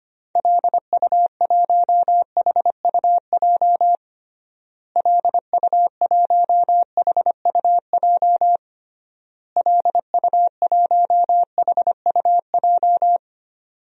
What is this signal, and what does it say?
Morse code, heard as a keyed tone, about 25 wpm, 700 Hz. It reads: LU1HUJ LU1HUJ LU1HUJ
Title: DXCC-100 Call Signs - 6 Characters - Repeated 3x - 25wpm